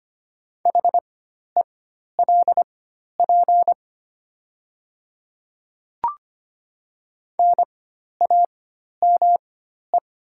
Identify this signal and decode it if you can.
Morse code, heard as a keyed tone, about 25 wpm, spaced as for 12 wpm, 700 Hz. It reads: HELP NAME
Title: 4x Character Spacing - Top 200 Words - 25wpm